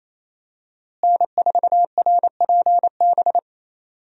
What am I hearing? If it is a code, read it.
Morse code, heard as a keyed tone, about 27 wpm, 700 Hz. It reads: N4RPB
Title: USA Call Signs - 5 Characters - Repeated 3x - 28wpm